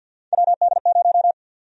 Morse code, heard as a keyed tone, about 47 wpm, 700 Hz. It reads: WD0
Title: USA Call Sign Prefixes - 50wpm